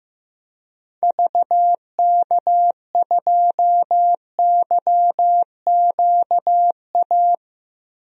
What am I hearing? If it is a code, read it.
Morse code, heard as a keyed tone, about 15 wpm, 700 Hz. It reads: VK2YQA
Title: DXCC-100 Call Signs - 6 Characters - Repeated 2x - 15wpm